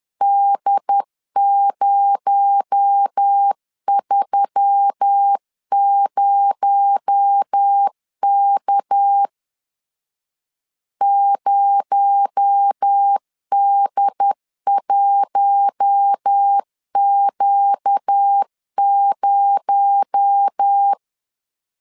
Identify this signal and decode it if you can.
Morse code, heard as a keyed tone, about 10 wpm, 800 Hz. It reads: D030K 0D1Q0
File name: LEZIONI / 6 PKX0 / 6 PKX0 10wpm 2707